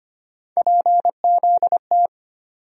Morse code, heard as a keyed tone, about 25 wpm, 700 Hz. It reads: PZT
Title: USA Call Sign Postfixes - 3 Letters 25wpm